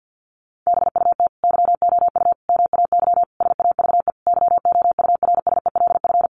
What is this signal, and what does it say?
Morse code, heard as a keyed tone, about 50 wpm, 700 Hz. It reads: BUT YOU MAY SAVE YOURSELF
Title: Sentences from Top 1100 Words 50wpm